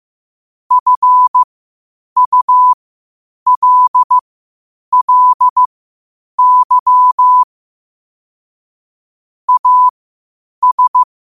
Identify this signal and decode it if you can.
Morse code, heard as a keyed tone, about 15 wpm, spaced as for 8 wpm, 1000 Hz. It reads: FULLY AS
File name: qotdfiles / 8wpm